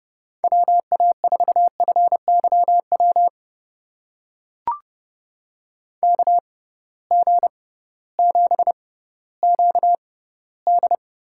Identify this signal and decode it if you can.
Morse code, heard as a keyed tone, about 32 wpm, spaced as for 17 wpm, 700 Hz. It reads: WA4FYW KG7QD
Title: USA Call Signs - 6 Characters - 6x Character Spacing - 30wpm